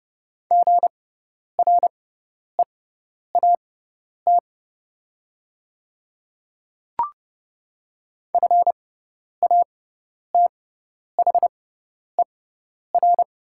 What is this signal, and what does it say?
Morse code, heard as a keyed tone, about 30 wpm, spaced as for 10 wpm, 700 Hz. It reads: GREAT FATHER
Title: 6x Character Spacing - Top 200 Words - 30wpm